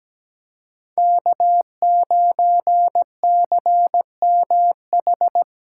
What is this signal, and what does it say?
Morse code, heard as a keyed tone, about 17 wpm, 700 Hz. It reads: K9CMH
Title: USA Call Signs - 5 Characters - 3x Character Spacing - 17wpm